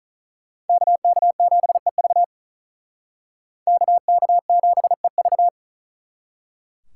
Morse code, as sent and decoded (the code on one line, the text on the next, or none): -.- -.- --... ...- / -.- -.- --... ...-
KK7EV KK7EV